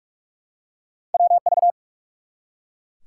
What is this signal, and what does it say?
Morse code, heard as a keyed tone, about 44 wpm, 700 Hz. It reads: WV
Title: US States and Canadian Provinces 45wpm